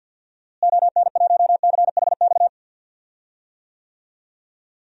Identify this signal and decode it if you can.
Morse code, heard as a keyed tone, about 46 wpm, 700 Hz. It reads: ON1CHX
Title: DXCC-100 Call Signs - 6 Characters - Repeated 2x - 50wpm